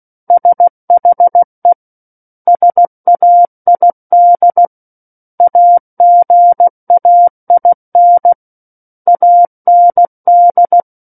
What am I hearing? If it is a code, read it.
Morse code, heard as a keyed tone, about 16 wpm, 700 Hz. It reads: SHE SAID AGAIN AND
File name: Book One / 07-Chapter 7 - 16 WPM